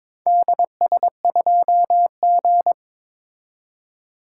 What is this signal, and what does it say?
Morse code, heard as a keyed tone, about 23 wpm, 700 Hz. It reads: DS2G
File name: practice / 22wpm / Four Letter-Number Combinations - 22wpm